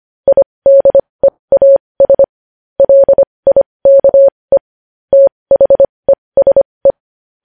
Morse code, heard as a keyed tone, about 25 wpm, spaced as for 20 wpm, 550 Hz. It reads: IDEAS LIKE THESE
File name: podcast / News headlines-1x-25x20